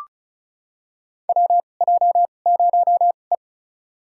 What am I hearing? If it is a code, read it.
Morse code, heard as a keyed tone, about 36 wpm, spaced as for 25 wpm, 700 Hz. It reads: WJ0E
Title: USA Call Signs - 4 Characters - 2x Character Spacing - 35wpm